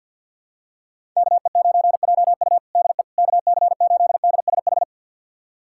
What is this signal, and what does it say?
Morse code, heard as a keyed tone, about 52 wpm, 690 Hz. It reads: KE9JU DE KC8DSH
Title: USA Callsign DE Callsign 50wpm